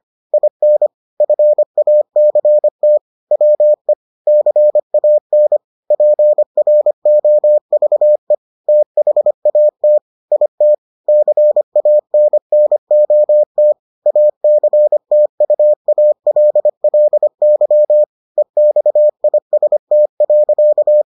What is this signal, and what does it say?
Morse code, heard as a keyed tone, about 25 wpm, 600 Hz. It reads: IN FACT WE CAN PROVE THAT IT CANNOT ACTUALLY EXIST.